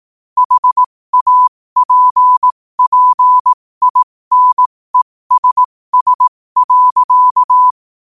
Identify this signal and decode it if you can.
Morse code, heard as a keyed tone, about 18 wpm, spaced as for 15 wpm, 1000 Hz. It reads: HAPPINESS.
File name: qotdfiles / 15wpm